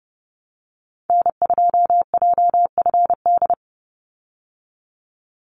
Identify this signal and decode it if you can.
Morse code, heard as a keyed tone, about 30 wpm, 700 Hz. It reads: N2JFD
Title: Call Signs Repeated 2x - 30wpm